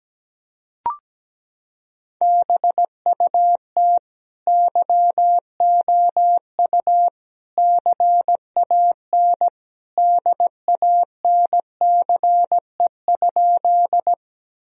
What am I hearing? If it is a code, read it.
Morse code, heard as a keyed tone, about 17 wpm, 700 Hz. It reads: BUT YOU CAN DANCE?
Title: Sentences from Top 1000 Words - Review Part 4 - 17wpm